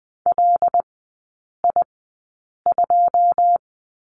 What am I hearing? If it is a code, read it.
Morse code, heard as a keyed tone, about 19 wpm, spaced as for 8 wpm, 700 Hz. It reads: LI2
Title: 133 Lesson 22e - Question Mark - Three Characters 20wpm